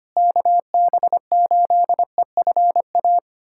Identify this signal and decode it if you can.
Morse code, heard as a keyed tone, about 25 wpm, 700 Hz. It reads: KB8EFA
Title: USA Callsign DE Callsign - 6 Characters - 25wpm